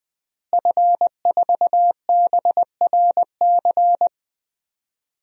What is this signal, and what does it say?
Morse code, heard as a keyed tone, about 21 wpm, 700 Hz. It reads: F4BRC
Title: DXCC-100 Call Signs - 5 Characters - Repeated 2x - 20wpm